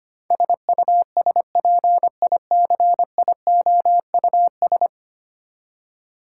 Morse code, heard as a keyed tone, about 25 wpm, 700 Hz. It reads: SUSPICIOUS
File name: 10 Letter Words - First 3 Letters Given - 25wpm